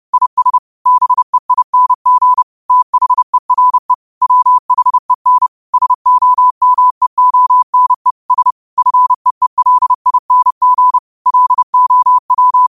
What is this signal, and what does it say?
Morse code, heard as a keyed tone, about 30 wpm, 1000 Hz. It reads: IS BEING THERE WHEN SOMEONE S FEELING LOW
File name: qotdfiles / 30wpm